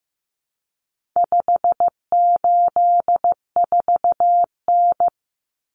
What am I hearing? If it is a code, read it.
Morse code, heard as a keyed tone, about 15 wpm, 700 Hz. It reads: 584N
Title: Common Cut Numbers - 4-Digit 15wpm